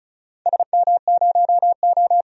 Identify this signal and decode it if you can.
Morse code, heard as a keyed tone, about 34 wpm, 700 Hz. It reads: SM0O
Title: DXCC-100 Call Signs - 4 Characters - 35wpm